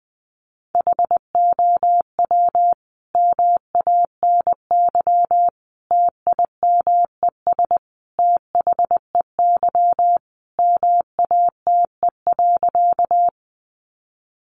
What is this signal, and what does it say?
Morse code, heard as a keyed tone, about 20 wpm, 700 Hz. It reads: HOW MANY TIMES THEY MATE.
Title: A Sound Of Thunder 20wpm